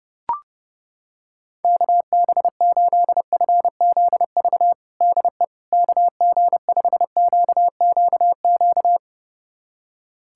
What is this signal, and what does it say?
Morse code, heard as a keyed tone, about 30 wpm, 700 Hz. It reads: KB8FZV DE KG5QQQ
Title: USA Callsign DE Callsign - 6 Characters - 30wpm